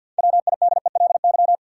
Morse code, heard as a keyed tone, about 52 wpm, 700 Hz. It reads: WIDELY
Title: Sentences from Top 1200 Words - New Word Drill - 50wpm